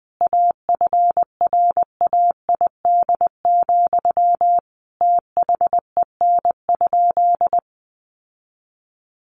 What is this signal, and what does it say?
Morse code, heard as a keyed tone, about 20 wpm, 700 Hz. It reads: AFRAID, THEN?
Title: Sentences from Top 1500 Words 20wpm